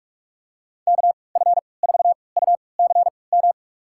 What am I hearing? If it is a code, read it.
Morse code, heard as a keyed tone, about 46 wpm, spaced as for 25 wpm, 700 Hz. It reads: KF4UCM
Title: USA Call Signs - 6 Characters - 3x Character Spacing - 45wpm